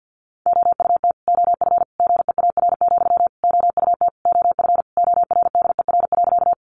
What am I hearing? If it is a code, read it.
Morse code, heard as a keyed tone, about 50 wpm, 700 Hz. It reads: OUT OF GEAR, OUT OF ORDER.
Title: Sentences from Top 1400 Words 50wpm